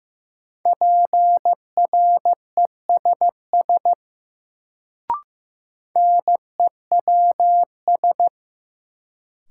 Morse code, heard as a keyed tone, about 15 wpm, 700 Hz. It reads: PRESS NEWS
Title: Sentences from Top 800 Words - New Word Drill - 15wpm